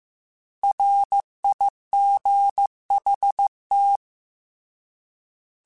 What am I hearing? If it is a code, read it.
Morse code, heard as a keyed tone, about 15 wpm, 800 Hz. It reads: RIGHT
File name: Misc / Misc-mp3s / Words @13